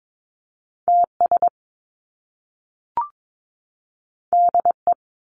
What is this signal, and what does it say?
Morse code, heard as a keyed tone, about 22 wpm, 700 Hz. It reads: TS DE
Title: Top Two Letter Combinations in Words 22wpm